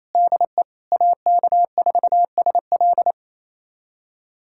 Morse code, heard as a keyed tone, about 28 wpm, 700 Hz. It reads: DE AK4SL
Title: USA Callsign DE Callsign - 5 Characters - 28wpm